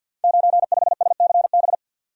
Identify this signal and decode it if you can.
Morse code, heard as a keyed tone, about 49 wpm, 700 Hz. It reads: 95SCB
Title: Code Groups - 5 Letter:Numbers - 50wpm